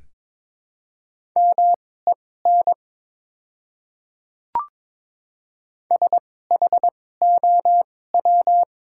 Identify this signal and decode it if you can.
Morse code, heard as a keyed tone, about 23 wpm, spaced as for 16 wpm, 700 Hz. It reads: MEN SHOW
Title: 2x Character Spacing - Top 300 Words - 22wpm